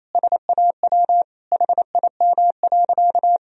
Morse code, heard as a keyed tone, about 28 wpm, 700 Hz. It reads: SAW HIM.
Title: Sentences from Top 1000 Words - Review Part 4 - 28wpm